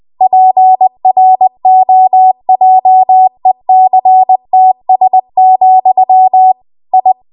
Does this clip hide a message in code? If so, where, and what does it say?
Yes: Morse code from 0.2 to 7.1 s, heard as a keyed tone, about 20 wpm, 750 Hz. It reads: PROJECTS, I